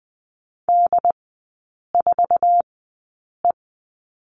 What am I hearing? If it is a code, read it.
Morse code, heard as a keyed tone, about 20 wpm, spaced as for 8 wpm, 690 Hz. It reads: D4E
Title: 166 Lesson 27e - Number 7 - Three Characters 20wpm